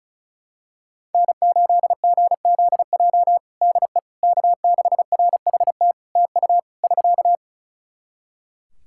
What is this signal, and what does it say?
Morse code, heard as a keyed tone, about 36 wpm, 700 Hz. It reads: N8GZJ DE K6RHT TU <SK>
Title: Practice QSO - Exchange 4 - Ending - 35wpm